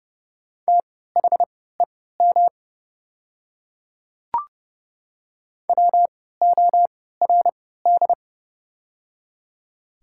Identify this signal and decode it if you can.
Morse code, heard as a keyed tone, about 30 wpm, spaced as for 17 wpm, 690 Hz. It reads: THEM WORD
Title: 3x Character Spacing - Top 100 Words - 30wpm